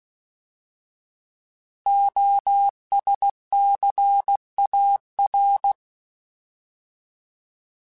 Morse code, heard as a keyed tone, about 16 wpm, 800 Hz. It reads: OSCAR
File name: Alphabetical / Alphabetical mp3s / Os @15